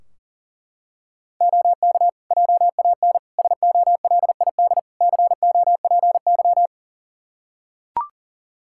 Morse code, heard as a keyed tone, about 40 wpm, 700 Hz. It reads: OK JAN SOLID COPY
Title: K1USN SST QSO Practice 40wpm